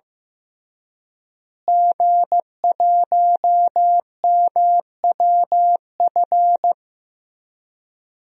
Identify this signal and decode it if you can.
Morse code, heard as a keyed tone, about 15 wpm, 700 Hz. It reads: G1MWF
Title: DXCC-100 Call Signs - 5 Characters - Repeated 3x - 15wpm